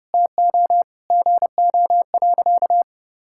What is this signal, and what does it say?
Morse code, heard as a keyed tone, about 30 wpm, 700 Hz. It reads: TO GO.
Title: Sentences from Top 1000 Words - Review Part 6 - 30wpm